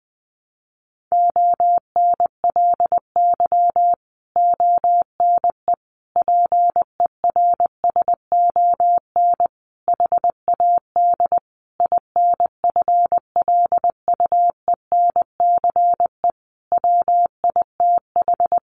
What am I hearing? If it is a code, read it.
Morse code, heard as a keyed tone, about 20 wpm, 700 Hz. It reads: ONLY ONE PERSON HAD INFLUENCE WITH
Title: Sentences from Top 1100 Words 20wpm